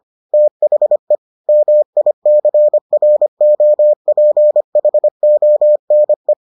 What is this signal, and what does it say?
Morse code, heard as a keyed tone, about 25 wpm, 600 Hz. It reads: THE MICROPHONE